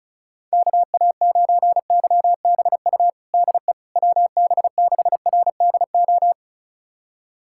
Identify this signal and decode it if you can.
Morse code, heard as a keyed tone, about 36 wpm, 700 Hz. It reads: KA9YBU DE WB6RDO
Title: USA Callsign DE Callsign 35wpm